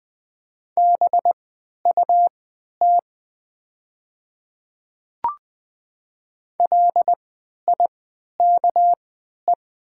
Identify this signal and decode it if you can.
Morse code, heard as a keyed tone, about 20 wpm, spaced as for 11 wpm, 700 Hz. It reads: BUT LIKE
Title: 3x Character Spacing - Top 100 Words - 20wpm